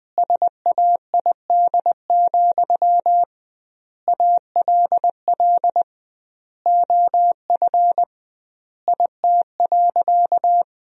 Morse code, heard as a keyed tone, about 20 wpm, 700 Hz. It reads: SAID, ALL OF IT.
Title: 2x Word-Spacing - Sentences from Top 500 Words - 20wpm